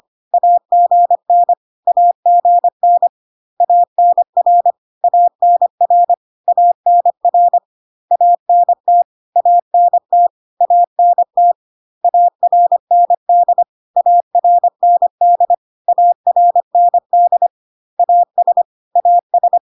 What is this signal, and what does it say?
Morse code, heard as a keyed tone, about 25 wpm, 700 Hz. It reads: AGN AGN ANR ANR ANR ANT ANT ANT ARND ARND ARND AS AS
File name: CW abbreviations alphabetical repeated 3 times 25 25wpm